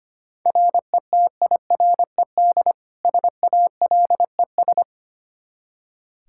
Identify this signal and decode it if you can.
Morse code, heard as a keyed tone, about 25 wpm, 700 Hz. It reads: RETIRED SALES
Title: Practice QSO - Exchange 3 - More Info - 25wpm